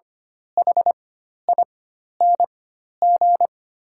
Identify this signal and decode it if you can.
Morse code, heard as a keyed tone, about 25 wpm, spaced as for 12 wpm, 690 Hz. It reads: HING
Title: 4x Character Spacing - Top 200 Words - 25wpm